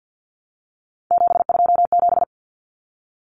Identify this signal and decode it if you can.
Morse code, heard as a keyed tone, about 50 wpm, 700 Hz. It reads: 727
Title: Numbers - 3-Digit 50wpm